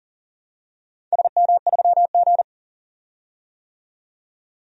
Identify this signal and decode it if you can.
Morse code, heard as a keyed tone, about 40 wpm, 700 Hz. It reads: SM3G
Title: DXCC-100 Call Signs - 4 Characters - 40wpm